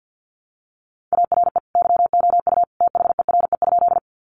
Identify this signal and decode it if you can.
Morse code, heard as a keyed tone, about 50 wpm, 700 Hz. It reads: ARE YOU THERE?
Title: Sentences from Top 100 Words 50wpm